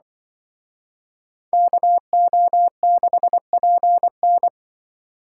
Morse code, heard as a keyed tone, about 23 wpm, 700 Hz. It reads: KO6PN
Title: Speed Racing - USA 2x2 Call Signs 20wpm